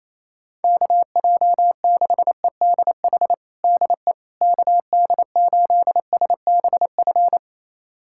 Morse code, heard as a keyed tone, about 28 wpm, 700 Hz. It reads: KJ6EDH DE KD8SBF
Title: USA Callsign DE Callsign - 6 Characters - 28wpm